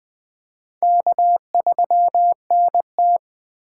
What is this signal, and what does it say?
Morse code, heard as a keyed tone, about 20 wpm, 700 Hz. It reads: K3NT
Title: USA Call Signs - 3x Character Spacing - 20wpm